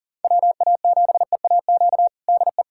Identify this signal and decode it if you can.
Morse code, heard as a keyed tone, about 40 wpm, 700 Hz. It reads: WA7EAQ DE
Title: USA Callsign DE Callsign 40wpm